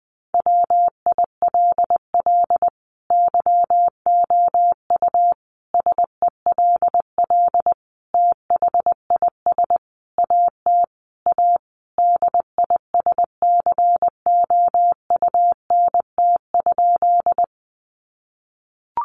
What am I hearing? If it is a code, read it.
Morse code, heard as a keyed tone, about 19 wpm, 700 Hz. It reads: WILL YOU SELL THIS AT A DISCOUNT?
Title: Sentences from Top 1200 Words 20wpm